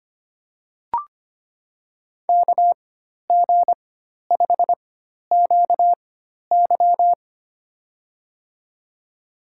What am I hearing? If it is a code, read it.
Morse code, heard as a keyed tone, about 25 wpm, spaced as for 12 wpm, 700 Hz. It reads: KG5QY
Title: USA Call Signs - 5 Characters - 4x Character Spacing - 25wpm